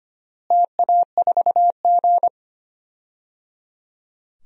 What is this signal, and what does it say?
Morse code, heard as a keyed tone, about 25 wpm, 700 Hz. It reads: TA4G
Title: DXCC-100 Call Signs - 4 Characters - Repeated 3x - 25wpm